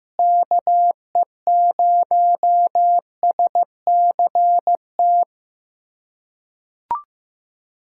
Morse code, heard as a keyed tone, about 15 wpm, 700 Hz. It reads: KE0SCT
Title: USA Callsign DE Callsign - 6 Characters - 15wpm